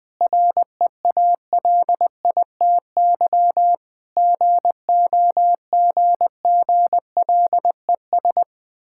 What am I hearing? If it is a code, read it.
Morse code, heard as a keyed tone, about 20 wpm, 700 Hz. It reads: REALITY GOGGLES
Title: Difficult Words with Hints - 20wpm